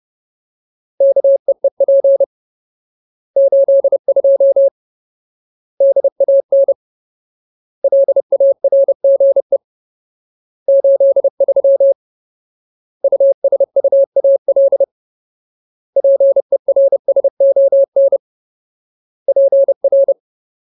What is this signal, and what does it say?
Morse code, heard as a keyed tone, about 30 wpm, 550 Hz. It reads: KEEP 82 DAN LARGE 83 USUAL PERSON PR
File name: Ultimate Word List / 30WPM Ultimate